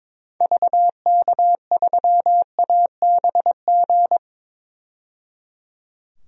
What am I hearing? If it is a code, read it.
Morse code, heard as a keyed tone, about 22 wpm, 700 Hz. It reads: VK3ABG
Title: DXCC-100 Call Signs - 6 Characters - 22wpm